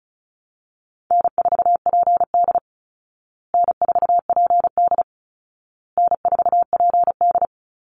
Morse code, heard as a keyed tone, about 34 wpm, 700 Hz. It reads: N4PD N4PD N4PD
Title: Call Signs Repeated 2x - 35wpm